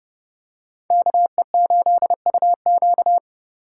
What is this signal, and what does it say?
Morse code, heard as a keyed tone, about 30 wpm, 700 Hz. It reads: KE8UQ